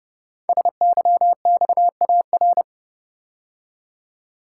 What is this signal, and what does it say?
Morse code, heard as a keyed tone, about 30 wpm, 700 Hz. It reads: SYXAR
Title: Code Groups - 5 Letters - 30wpm